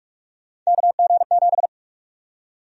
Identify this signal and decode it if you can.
Morse code, heard as a keyed tone, about 46 wpm, 700 Hz. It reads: KG7
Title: USA Call Sign Prefixes - 45wpm